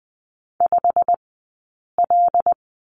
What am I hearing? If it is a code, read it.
Morse code, heard as a keyed tone, about 20 wpm, 700 Hz. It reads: 5 L
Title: 086 Lesson 15d - Number 5 - Two Characters 20wpm